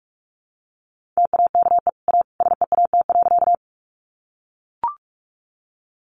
Morse code, heard as a keyed tone, about 44 wpm, 700 Hz. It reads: TAKE A SEAT.
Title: Sentences from Top 1500 Words 45wpm